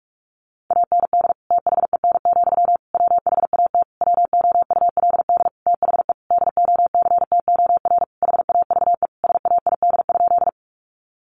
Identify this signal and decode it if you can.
Morse code, heard as a keyed tone, about 45 wpm, 700 Hz. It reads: AND THEN, WHAT WOULD THE DOCTOR HAVE SAID?
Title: Sentences from Top 1500 Words 45wpm